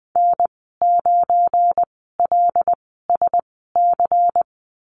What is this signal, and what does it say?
Morse code, heard as a keyed tone, about 19 wpm, spaced as for 14 wpm, 700 Hz. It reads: N9LSC
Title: 179 Lesson 29g - Slash - Call Signs 20wpm